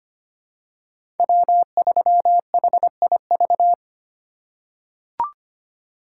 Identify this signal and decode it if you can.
Morse code, heard as a keyed tone, about 25 wpm, 700 Hz. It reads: W3HIV
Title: USA Call Signs - 5 Characters - 3x Character Spacing - 25wpm